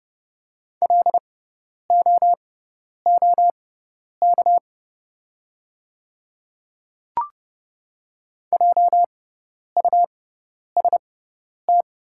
Morse code, heard as a keyed tone, about 30 wpm, spaced as for 10 wpm, 690 Hz. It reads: LOOK JUST
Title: 6x Character Spacing - Top 200 Words - 30wpm